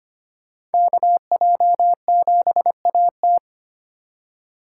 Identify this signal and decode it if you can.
Morse code, heard as a keyed tone, about 31 wpm, 700 Hz. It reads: KJ7AT